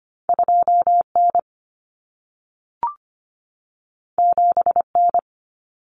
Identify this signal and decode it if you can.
Morse code, heard as a keyed tone, about 25 wpm, 700 Hz. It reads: 2N 7N
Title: Common Cut Numbers - 2-Digit 25wpm